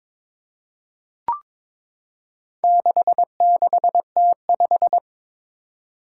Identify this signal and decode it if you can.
Morse code, heard as a keyed tone, about 22 wpm, 700 Hz. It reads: 66T5
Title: Four Challenging Letter-Number Combinations 22wpm